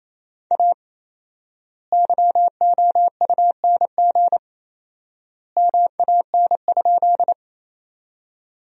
Morse code, heard as a keyed tone, about 28 wpm, 700 Hz. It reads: A YOUNG MAN?
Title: 4x Word-Spacing - Sentences from Top 400 Words - 28wpm